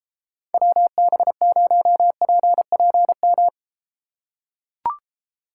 Morse code, heard as a keyed tone, about 33 wpm, 700 Hz. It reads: WB0PPM